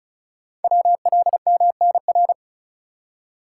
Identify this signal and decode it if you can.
Morse code, heard as a keyed tone, about 36 wpm, 690 Hz. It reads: WLMNR
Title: Code Groups - 5 Letters - 35wpm